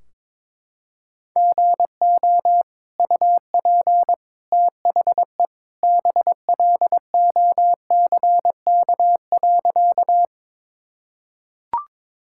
Morse code, heard as a keyed tone, about 22 wpm, 700 Hz. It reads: GO UP THE BLOCK.
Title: Sentences from Top 1600 Words 22wpm